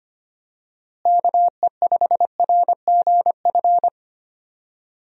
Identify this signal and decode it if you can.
Morse code, heard as a keyed tone, about 25 wpm, 700 Hz. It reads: KE5RGF
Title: USA Call Signs - 6 Characters - Repeated 2x - 25wpm